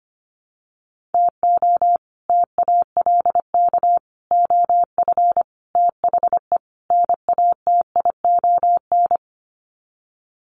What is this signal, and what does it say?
Morse code, heard as a keyed tone, about 25 wpm, 700 Hz. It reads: TO TALK OF THE NATION
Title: Sets of 5 Words 25wpm